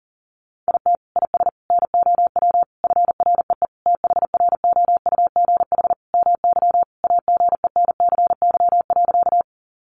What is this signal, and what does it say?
Morse code, heard as a keyed tone, about 40 wpm, 700 Hz. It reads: IT IS NOW FREE THROUGH MY AGENCY.
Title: Sentences from Top 1300 Words 40wpm